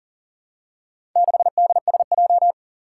Morse code, heard as a keyed tone, about 40 wpm, 700 Hz. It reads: 6DSJ
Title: Four Challenging Letter-Number Combinations 40wpm